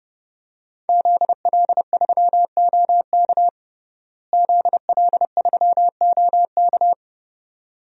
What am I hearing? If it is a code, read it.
Morse code, heard as a keyed tone, about 30 wpm, 700 Hz. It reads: ZL3OK ZL3OK
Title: DXCC-100 Call Signs - 5 Characters - Repeated 2x - 30wpm